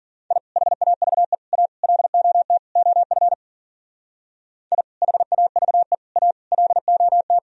Morse code, heard as a keyed tone, about 52 wpm, 700 Hz. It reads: I HAVE A LOT OF I HAVE A LOT